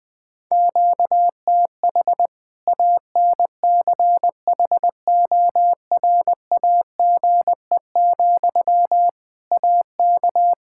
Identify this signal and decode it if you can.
Morse code, heard as a keyed tone, about 20 wpm, 700 Hz. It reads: QTH ANCHORAGE, AK
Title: K1USN SST QSO Practice 20wpm